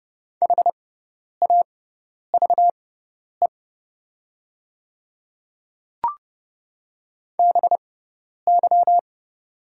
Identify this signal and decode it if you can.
Morse code, heard as a keyed tone, about 30 wpm, spaced as for 10 wpm, 700 Hz. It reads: HAVE BY
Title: 6x Character Spacing - Top 100 Words - 30wpm